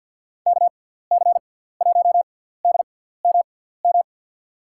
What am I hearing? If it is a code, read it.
Morse code, heard as a keyed tone, about 53 wpm, spaced as for 17 wpm, 690 Hz. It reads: KC1DMM
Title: USA Call Signs - 6x Character Spacing - 50wpm